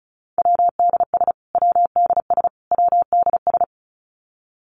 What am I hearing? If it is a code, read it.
Morse code, heard as a keyed tone, about 34 wpm, 700 Hz. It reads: WDS WDS WDS
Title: Abbreviations Repeated 3x - 35wpm